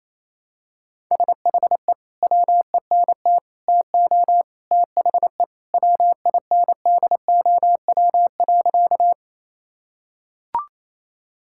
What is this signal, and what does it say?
Morse code, heard as a keyed tone, about 28 wpm, 700 Hz. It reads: SHE WENT TO THE WINDOW.
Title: Sentences from Top 1600 Words 28wpm